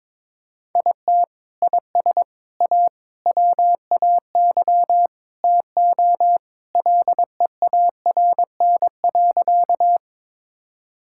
Morse code, heard as a keyed tone, about 22 wpm, 700 Hz. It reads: IT IS A WAY TO LEARN.
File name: practice / 22wpm / Sentences with 3-Word Phrases - 22wpm